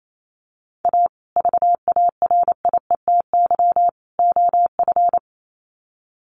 Morse code, heard as a keyed tone, about 28 wpm, 700 Hz. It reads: A VARIETY OF
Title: Sets of 3 Words 28wpm